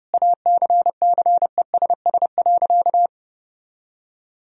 Morse code, heard as a keyed tone, about 30 wpm, 700 Hz. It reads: ACCESS.
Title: Sentences from Top 1000 Words - Review Part 4 - 30wpm